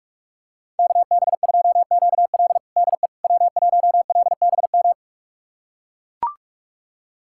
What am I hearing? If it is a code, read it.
Morse code, heard as a keyed tone, about 43 wpm, 700 Hz. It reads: KB2QL DE W1LBM
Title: USA Callsign DE Callsign - 5 Characters - 45wpm